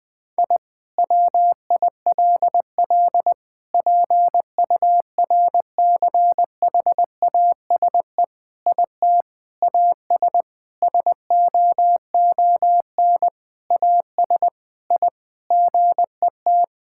Morse code, heard as a keyed tone, about 20 wpm, 700 Hz. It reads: I WILL PURCHASE IT AS SOON AS I GET
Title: Sentences from Top 1000 Words - Review Part 5 - 20wpm